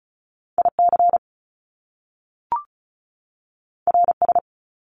Morse code, heard as a keyed tone, about 36 wpm, 700 Hz. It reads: IC RS
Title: Top Two Letter Combinations in Words 35wpm